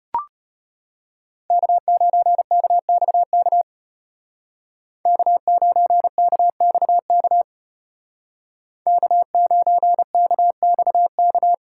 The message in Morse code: K9KXK K9KXK K9KXK

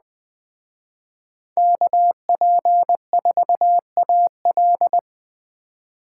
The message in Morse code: KP4AL

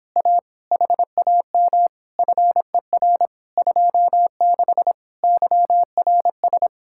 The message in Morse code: A HAM FER 26 YRS

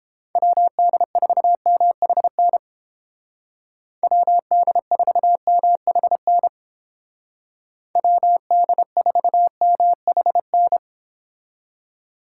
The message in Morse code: WD4MHN WD4MHN WD4MHN

